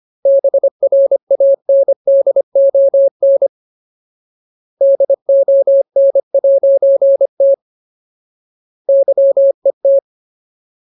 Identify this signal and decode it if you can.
Morse code, heard as a keyed tone, about 25 wpm, 550 Hz. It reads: BRANDON DON'T YET
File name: Ultimate Word List / 25WPM Ultimate